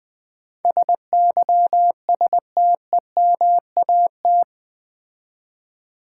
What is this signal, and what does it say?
Morse code, heard as a keyed tone, about 20 wpm, 700 Hz. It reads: SYSTEMAT
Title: Build A Word - 4 Letters At A Time - 10-Letter Words - 20wpm